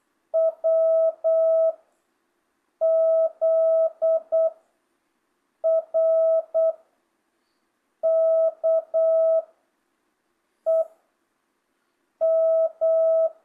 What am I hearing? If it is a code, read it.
Morse code, heard as a keyed tone, about 8 wpm, 660 Hz. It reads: W Z R K E M
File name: jh8cbh / 2025cw